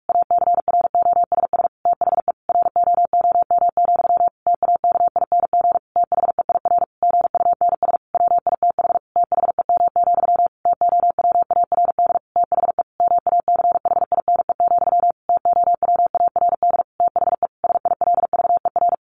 Morse code, heard as a keyed tone, about 45 wpm, 700 Hz. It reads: ACROSS THE ROOM, TAKING THEIR GUNS WITH THEM, TOWARD THE MACHINE, TOWARD THE SILVER
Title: A Sound Of Thunder 45wpm